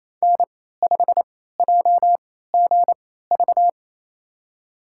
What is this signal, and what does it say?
Morse code, heard as a keyed tone, about 28 wpm, spaced as for 16 wpm, 700 Hz. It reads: N5JGV
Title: USA Call Signs - 3x Character Spacing - 28wpm